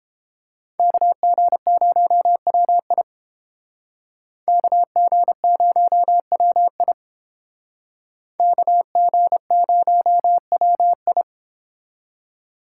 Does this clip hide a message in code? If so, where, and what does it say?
Yes: Morse code from 0.8 to 11.2 s, heard as a keyed tone, about 29 wpm, 700 Hz. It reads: KG0WI KG0WI KG0WI